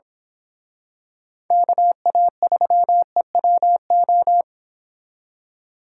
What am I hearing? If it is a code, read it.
Morse code, heard as a keyed tone, about 26 wpm, 700 Hz. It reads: KA3EWO
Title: Speed Racing - USA 2x3 Call Signs 22wpm